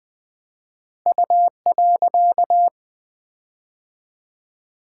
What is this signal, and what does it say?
Morse code, heard as a keyed tone, about 21 wpm, 700 Hz. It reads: U.
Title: 4x Word-Spacing - Sets of 3 Words - 20wpm